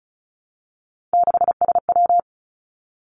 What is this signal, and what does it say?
Morse code, heard as a keyed tone, about 36 wpm, 700 Hz. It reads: 6SW
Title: Three Challenging Letter-Number Combinations 35wpm